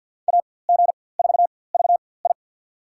Morse code, heard as a keyed tone, about 47 wpm, spaced as for 23 wpm, 700 Hz. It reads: AG4VI
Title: USA Call Signs - 5 Characters - 4x Character Spacing - 50wpm